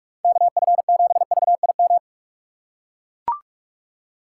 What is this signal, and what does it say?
Morse code, heard as a keyed tone, about 42 wpm, 700 Hz. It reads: KF7VIM
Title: USA Call Signs - 2x Character Spacing - 45wpm